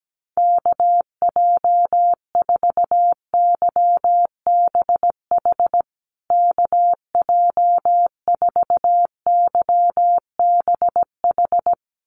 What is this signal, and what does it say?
Morse code, heard as a keyed tone, about 17 wpm, 700 Hz. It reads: KJ4YBH KJ4YBH
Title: USA Call Signs Repeated 2x 17wpm